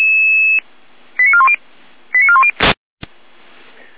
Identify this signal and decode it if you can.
Five-tone selective call (ZVEI), heard as quick runs of five tones, tones of about 70 ms. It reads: 89410 89410
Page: SDR/ZVEI